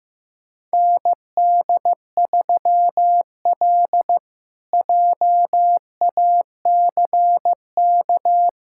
Morse code, heard as a keyed tone, about 15 wpm, 700 Hz. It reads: ND3L JACK
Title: K1USN SST QSO Practice - 15wpm